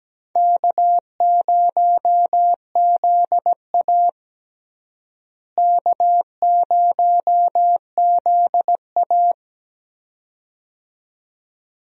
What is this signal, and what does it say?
Morse code, heard as a keyed tone, about 17 wpm, 700 Hz. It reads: K0ZA K0ZA
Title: USA Call Signs - 4 Characters - Repeated 2x - 17wpm